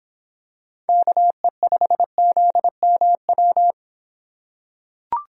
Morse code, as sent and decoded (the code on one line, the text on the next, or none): -.- . ..... --.. -- .--
KE5ZMW